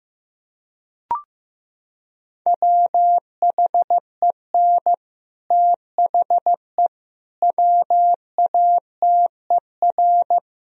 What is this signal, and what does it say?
Morse code, heard as a keyed tone, about 16 wpm, 700 Hz. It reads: WHEN THE WATER
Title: Sentences from Top 1000 Words - Review Part 1 - 15wpm